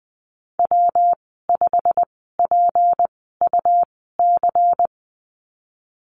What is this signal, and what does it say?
Morse code, heard as a keyed tone, about 20 wpm, spaced as for 15 wpm, 700 Hz. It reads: W5PUC
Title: 186 Lesson 30g - Letter b - Call Signs 20wpm